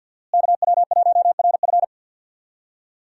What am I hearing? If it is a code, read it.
Morse code, heard as a keyed tone, about 49 wpm, 700 Hz. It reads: KW1RF
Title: USA Call Signs - 5 Characters - Repeated 2x - 50wpm